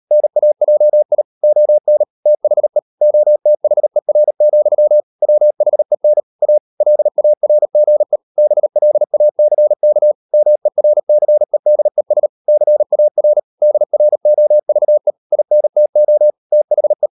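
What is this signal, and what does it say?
Morse code, heard as a keyed tone, about 38 wpm, 600 Hz. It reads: NAJI ON THE OTHER, WHEN A LARGE BLACK MERCEDES CAR DROVE INTO THE
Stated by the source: Morse code practice at 35–39 wpm sending English text of Ken Follett's novel Never, from CW med Gnister.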